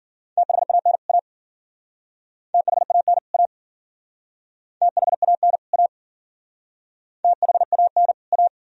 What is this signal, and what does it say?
Morse code, heard as a keyed tone, about 48 wpm, 700 Hz. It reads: THAN A THAN A THAN A THAN A
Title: Speed-Racing - Sets of 2 Words 40wpm